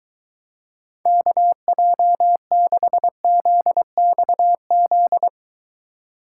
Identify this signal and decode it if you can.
Morse code, heard as a keyed tone, about 23 wpm, 700 Hz. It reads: KJ6ZXZ